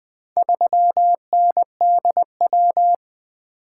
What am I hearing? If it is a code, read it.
Morse code, heard as a keyed tone, about 20 wpm, 700 Hz. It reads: 3NDW
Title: Four Letter-Number Combinations - 20wpm